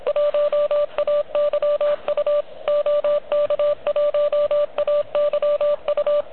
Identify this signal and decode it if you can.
Morse code, heard as a keyed tone, about 22 wpm, 590 Hz. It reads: 1AYU OK1AYU
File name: OK QRP 2022 / Zvuk / REC37